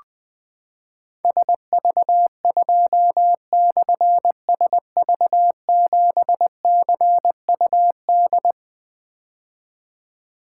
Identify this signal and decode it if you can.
Morse code, heard as a keyed tone, about 20 wpm, 700 Hz. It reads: SV2/SV7CUD
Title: Difficult Call Signs 20wpm